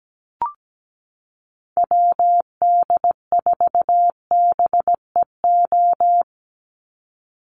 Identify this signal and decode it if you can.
Morse code, heard as a keyed tone, about 17 wpm, 700 Hz. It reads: WD4BEO
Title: USA Call Signs 17wpm